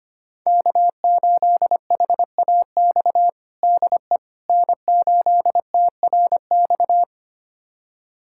Morse code, heard as a keyed tone, about 25 wpm, 700 Hz. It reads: K8HAX DE N8TRX
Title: USA Callsign DE Callsign - 5 Characters - 25wpm